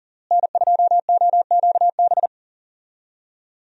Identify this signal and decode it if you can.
Morse code, heard as a keyed tone, about 38 wpm, 700 Hz. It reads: N2OQB
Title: USA Call Signs - 5 Characters - Repeated 3x - 40wpm